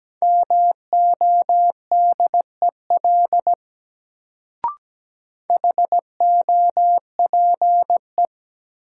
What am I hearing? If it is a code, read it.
Morse code, heard as a keyed tone, about 17 wpm, 700 Hz. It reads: MODEL HOPE
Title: Sentences from Top 1000 Words - New Word Drill - 17wpm